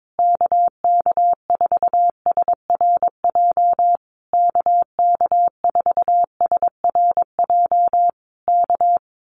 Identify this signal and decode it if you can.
Morse code, heard as a keyed tone, about 22 wpm, 700 Hz. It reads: KK4SRJ KK4SRJ K
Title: Common QSO Phrases Encore 22wpm